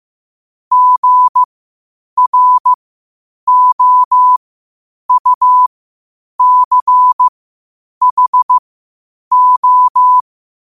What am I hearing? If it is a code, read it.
Morse code, heard as a keyed tone, about 15 wpm, spaced as for 8 wpm, 1000 Hz. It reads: GROUCHO
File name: qotdfiles / 8wpm